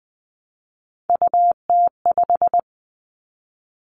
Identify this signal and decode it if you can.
Morse code, heard as a keyed tone, about 20 wpm, 700 Hz. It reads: UT5
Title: Callsign Prefixes for Top 25 DXCC - 20wpm